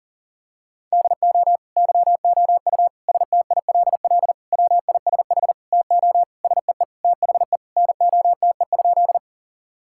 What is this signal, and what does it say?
Morse code, heard as a keyed tone, about 40 wpm, 700 Hz. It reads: DO YOU STILL WISH TO SEE THE NOTE?
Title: Sentences from Top 1600 Words 40wpm